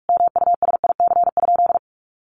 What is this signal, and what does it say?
Morse code, heard as a keyed tone, about 44 wpm, 700 Hz. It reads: MUSIC?
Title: Sentences from Top 1200 Words 45wpm